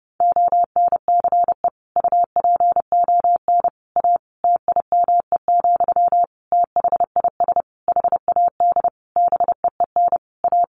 Morse code, heard as a keyed tone, about 30 wpm, 700 Hz. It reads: ONCE UPON A TIME, THIS HAD BEEN A
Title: Sentences from Top 1500 Words 30wpm